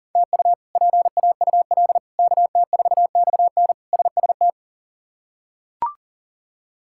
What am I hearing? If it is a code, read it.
Morse code, heard as a keyed tone, about 40 wpm, 700 Hz. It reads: TU PAUL KT4XN SST
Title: K1USN SST QSO Practice - 40wpm